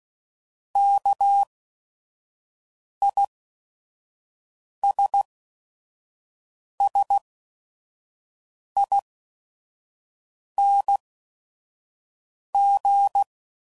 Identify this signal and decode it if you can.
Morse code, heard as a keyed tone, about 16 wpm, spaced as for 5 wpm, 800 Hz. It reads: KISSING
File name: Suffixes / Special ING's / VerySlowToFaster ING's